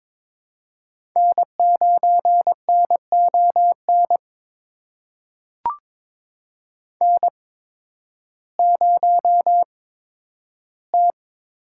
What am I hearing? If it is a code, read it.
Morse code, heard as a keyed tone, about 22 wpm, spaced as for 12 wpm, 700 Hz. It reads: N9NON N0T
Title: USA Call Signs - 5 Characters - 8x Character Spacing - 22wpm